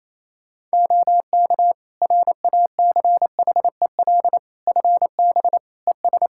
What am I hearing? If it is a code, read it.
Morse code, heard as a keyed tone, about 27 wpm, 700 Hz. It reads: OK RACHEL FB ES
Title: K1USN SST QSO Practice 28wpm